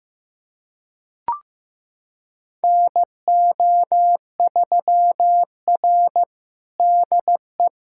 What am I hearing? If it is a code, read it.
Morse code, heard as a keyed tone, about 15 wpm, 700 Hz. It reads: NO3R DE
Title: USA Callsign DE Callsign - 4 Characters - 15wpm